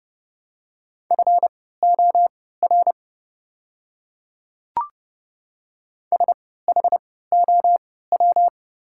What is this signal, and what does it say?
Morse code, heard as a keyed tone, about 30 wpm, spaced as for 17 wpm, 700 Hz. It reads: FOR SHOW
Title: 3x Character Spacing - Top 300 Words - 30wpm